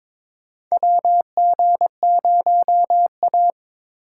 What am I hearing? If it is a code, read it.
Morse code, heard as a keyed tone, about 22 wpm, 700 Hz. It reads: WG0A
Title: USA Call Signs - 4 Characters - 3x Character Spacing - 22wpm